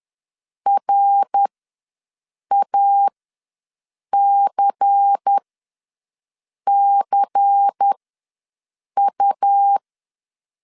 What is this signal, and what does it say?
Morse code, heard as a keyed tone, about 10 wpm, spaced as for 6 wpm, 800 Hz. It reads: RACCU
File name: LEZIONI / 5 LBVC / 5 LBVC ws 10wpm 3733